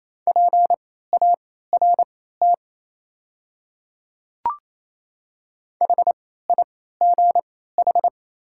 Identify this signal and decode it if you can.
Morse code, heard as a keyed tone, about 27 wpm, spaced as for 16 wpm, 690 Hz. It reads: PART HIGH
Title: 3x Character Spacing - Top 300 Words - 28wpm